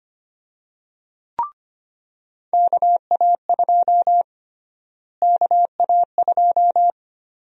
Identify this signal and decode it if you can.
Morse code, heard as a keyed tone, about 25 wpm, 700 Hz. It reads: KA2 KA2
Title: USA Call Sign Prefixes - Repeated 2x - 25wpm